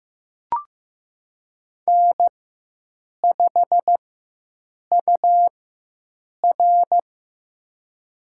Morse code, heard as a keyed tone, about 15 wpm, spaced as for 7 wpm, 700 Hz. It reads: N5UR
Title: USA Call Signs - 4 Characters - 4x Character Spacing - 15wpm